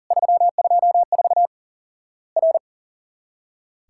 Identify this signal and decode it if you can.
Morse code, heard as a keyed tone, about 40 wpm, 700 Hz. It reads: 324 R
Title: Worked All Europe DX Contest - QTC Exchange - 40wpm